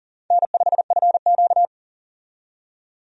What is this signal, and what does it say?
Morse code, heard as a keyed tone, about 40 wpm, 700 Hz. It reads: N5FQ
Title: USA Call Signs - 4 Characters - 40wpm